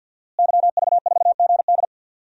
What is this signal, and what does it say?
Morse code, heard as a keyed tone, about 53 wpm, 700 Hz. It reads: YV4GD
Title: DXCC-100 Call Signs - 5 Characters - Repeated 2x - 50wpm